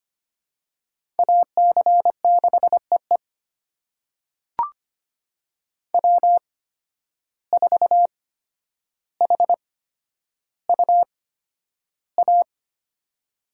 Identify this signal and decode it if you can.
Morse code, heard as a keyed tone, about 25 wpm, spaced as for 11 wpm, 700 Hz. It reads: AC6EE W4HUA
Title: USA Call Signs - 5 Characters - 8x Character Spacing - 25wpm